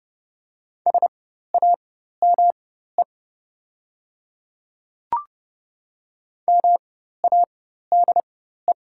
Morse code, heard as a keyed tone, about 30 wpm, spaced as for 14 wpm, 700 Hz. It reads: SAME MADE
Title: 4x Character Spacing - Top 200 Words - 30wpm